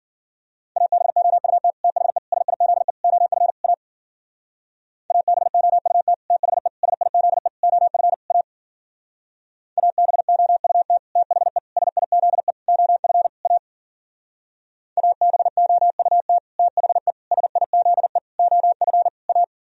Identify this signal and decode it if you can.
Morse code, heard as a keyed tone, about 51 wpm, 700 Hz. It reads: ABOUT THE SIZE OF A ABOUT THE SIZE OF A ABOUT THE SIZE OF A ABOUT THE SIZE OF A